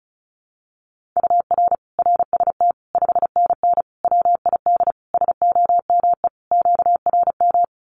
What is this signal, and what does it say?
Morse code, heard as a keyed tone, about 34 wpm, 700 Hz. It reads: UR RST 5NN WID SOME QRM
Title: Common QSO Phrases 35wpm